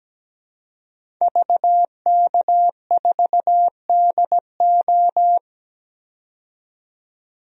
Morse code, heard as a keyed tone, about 17 wpm, 700 Hz. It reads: VK4DO
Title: DXCC-100 Call Signs - 5 Characters - 17wpm